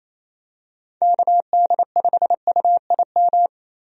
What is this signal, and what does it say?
Morse code, heard as a keyed tone, about 28 wpm, 700 Hz. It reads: KD5UIM
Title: USA Call Signs - 6 Characters - 28wpm